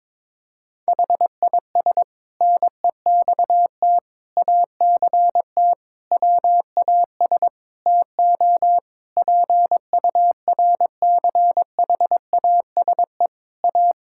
Morse code, heard as a keyed tone, about 22 wpm, 700 Hz. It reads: HIS NEXT ACT WAS TO PURCHASE A
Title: Sentences from Top 1000 Words - Review Part 5 - 22wpm